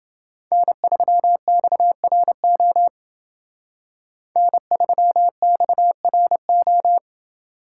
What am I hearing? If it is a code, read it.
Morse code, heard as a keyed tone, about 29 wpm, 700 Hz. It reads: N3XRO N3XRO